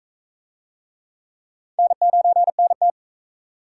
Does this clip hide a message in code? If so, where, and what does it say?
Yes: Morse code from 1.8 to 2.9 s, heard as a keyed tone, about 39 wpm, 700 Hz. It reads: N9NT